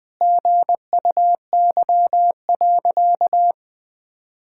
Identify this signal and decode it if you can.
Morse code, heard as a keyed tone, about 21 wpm, 700 Hz. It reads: GUY.
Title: Sentences with 8-Word Phrases - 20wpm